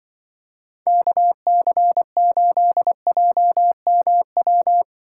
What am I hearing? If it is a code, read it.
Morse code, heard as a keyed tone, about 24 wpm, 700 Hz. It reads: KC8JMW
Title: Speed Racing - USA Call Signs 20wpm